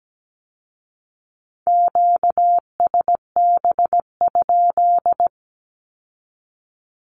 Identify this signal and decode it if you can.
Morse code, heard as a keyed tone, about 17 wpm, 700 Hz. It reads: QSB?
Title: Q-Codes with Spoken Meaning - 17wpm